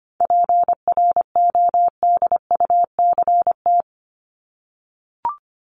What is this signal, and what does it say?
Morse code, heard as a keyed tone, about 25 wpm, 700 Hz. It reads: PRODUCT
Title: Top 500 Words Repeated 25wpm